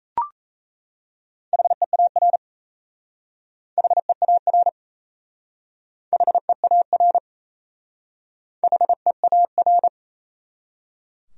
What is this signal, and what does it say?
Morse code, heard as a keyed tone, about 36 wpm, 700 Hz. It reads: HEAR HEAR HEAR HEAR